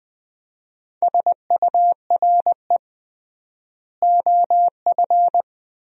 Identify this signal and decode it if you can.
Morse code, heard as a keyed tone, about 20 wpm, 700 Hz. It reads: SURE OF
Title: 3x Word-Spacing - Sentences from Top 300 Words - 20wpm